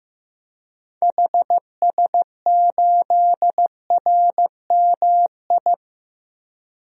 Morse code, heard as a keyed tone, about 15 wpm, 700 Hz. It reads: HS8RMI
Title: DXCC-100 Call Signs - 6 Characters - Repeated 3x - 15wpm